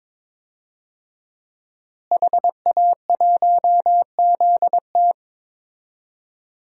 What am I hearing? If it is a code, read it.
Morse code, heard as a keyed tone, about 22 wpm, 700 Hz. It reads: HA1ZT
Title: DXCC-100 Call Signs - 5 Characters - Repeated 2x - 22wpm